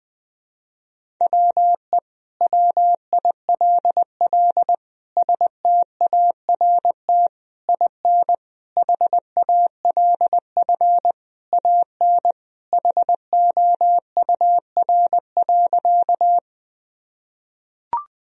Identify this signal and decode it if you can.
Morse code, heard as a keyed tone, about 20 wpm, 700 Hz. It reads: WE WILL START IN HALF AN HOUR.
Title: Sentences from Top 1000 Words - Review Part 1 - 20wpm